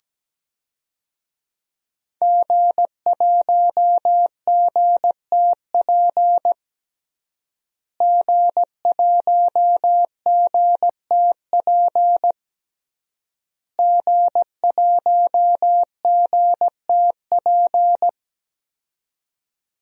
Morse code, heard as a keyed tone, about 17 wpm, 700 Hz. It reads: G1GTP G1GTP G1GTP
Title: DXCC-100 Call Signs - 5 Characters - Repeated 3x - 17wpm